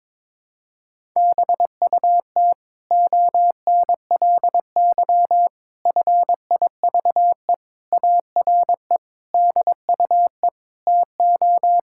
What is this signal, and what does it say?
Morse code, heard as a keyed tone, about 21 wpm, 700 Hz. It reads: BUT ONLY FIVE ARE DUE TO
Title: Sentences from Top 1000 Words - Review Part 6 - 22wpm